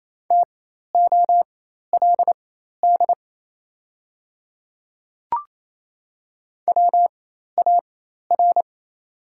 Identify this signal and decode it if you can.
Morse code, heard as a keyed tone, about 28 wpm, spaced as for 13 wpm, 700 Hz. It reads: TOLD WAR
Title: 4x Character Spacing - Top 500 Words - 28wpm